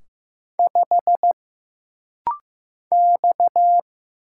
Morse code, heard as a keyed tone, about 15 wpm, 700 Hz. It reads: 5 X
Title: Single Letter-Number Alt - Rapid-Fire 15wpm